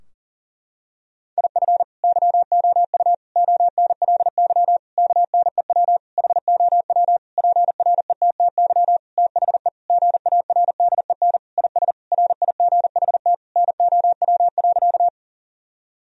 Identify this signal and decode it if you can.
Morse code, heard as a keyed tone, about 40 wpm, 700 Hz. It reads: IF YOU ONLY KNEW HOW PRETTY THE GARDEN IS RIGHT NOW.
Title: Sentences from Top 1000 Words - Review Part 4 - 40wpm